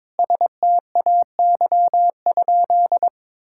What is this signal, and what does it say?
Morse code, heard as a keyed tone, about 21 wpm, 700 Hz. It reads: STAY?
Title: Sentences from Top 1600 Words 22wpm